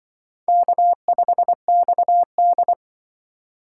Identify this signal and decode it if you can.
Morse code, heard as a keyed tone, about 24 wpm, 700 Hz. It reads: K5XD
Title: Speed Racing - USA Call Signs 20wpm